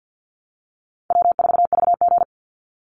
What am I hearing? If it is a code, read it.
Morse code, heard as a keyed tone, about 50 wpm, 700 Hz. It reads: W4VG
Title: Call Signs Repeated 3x - 50wpm